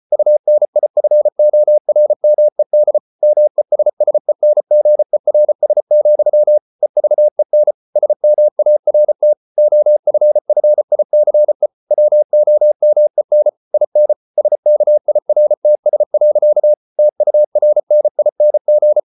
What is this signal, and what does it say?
Morse code, heard as a keyed tone, about 34 wpm, 600 Hz. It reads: UNIFORMED MESSENGERS, EVEN SMART OFFICE WOMEN IN SKIRTS. TURNING